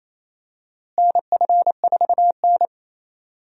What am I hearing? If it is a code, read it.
Morse code, heard as a keyed tone, about 28 wpm, 700 Hz. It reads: NF4N
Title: USA Call Signs - 4 Characters - Repeated 2x - 28wpm